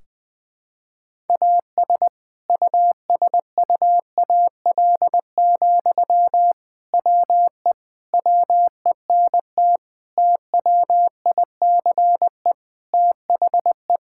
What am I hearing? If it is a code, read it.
Morse code, heard as a keyed tone, about 20 wpm, 700 Hz. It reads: AS USUAL, WE WENT TWICE THE
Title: Sentences from Top 1600 Words 20wpm